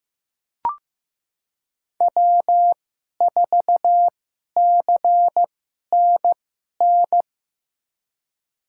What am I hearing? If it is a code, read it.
Morse code, heard as a keyed tone, about 15 wpm, spaced as for 11 wpm, 700 Hz. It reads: W4CNN
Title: USA Call Signs - 5 Characters - 2x Character Spacing - 15wpm